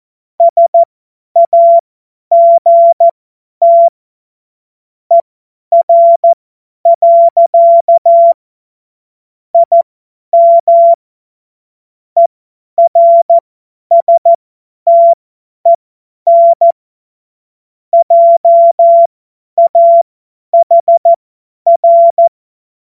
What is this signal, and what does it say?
Morse code, heard as a keyed tone, about 14 wpm, spaced as for 10 wpm, 680 Hz. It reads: SAGT ER. IM ERSTEN JAHR
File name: Morsekurs / Witze / Witz 05 März 2024 50 F70